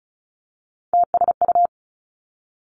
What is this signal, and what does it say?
Morse code, heard as a keyed tone, about 36 wpm, 700 Hz. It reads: TSU
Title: Three Challenging Letter-Number Combinations 35wpm